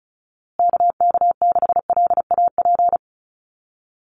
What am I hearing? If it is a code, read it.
Morse code, heard as a keyed tone, about 34 wpm, 700 Hz. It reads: KK6LAP
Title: USA Call Signs Repeated 2x 35wpm